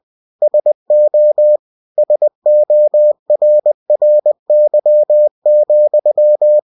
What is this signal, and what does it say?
Morse code, heard as a keyed tone, about 20 wpm, 600 Hz. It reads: SO SORRY,